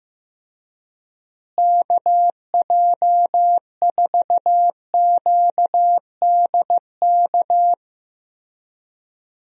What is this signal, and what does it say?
Morse code, heard as a keyed tone, about 15 wpm, 700 Hz. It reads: KJ4QDK
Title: USA Call Signs - 6 Characters - Repeated 3x - 15wpm